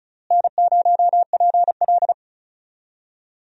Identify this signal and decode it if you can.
Morse code, heard as a keyed tone, about 36 wpm, 700 Hz. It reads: N0PL
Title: USA Call Signs - 4 Characters - Repeated 2x - 35wpm